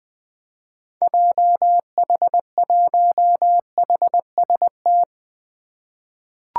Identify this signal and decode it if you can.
Morse code, heard as a keyed tone, about 20 wpm, 700 Hz. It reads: JH1HST
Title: 2023 POTA Activator Call Signs - 20wpm